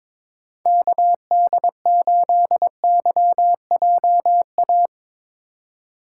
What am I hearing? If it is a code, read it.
Morse code, heard as a keyed tone, about 21 wpm, 700 Hz. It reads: KD8YJA
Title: USA Call Signs - 6 Characters - 2x Character Spacing - 22wpm